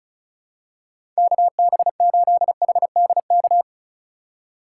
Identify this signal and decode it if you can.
Morse code, heard as a keyed tone, about 36 wpm, 700 Hz. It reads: KB8HDK
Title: USA Call Signs - 6 Characters - 3x Character Spacing - 35wpm